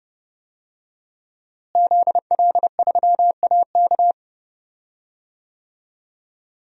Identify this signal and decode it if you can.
Morse code, heard as a keyed tone, about 31 wpm, 700 Hz. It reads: ZL3AK
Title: DXCC-100 Call Signs - 5 Characters - Repeated 2x - 30wpm